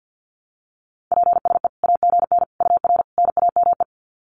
Morse code, heard as a keyed tone, about 50 wpm, 700 Hz. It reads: PSE AGN UR NAME
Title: Common QSO Phrases 50wpm